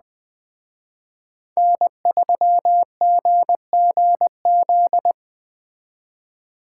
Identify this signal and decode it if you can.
Morse code, heard as a keyed tone, about 21 wpm, 700 Hz. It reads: N3GGZ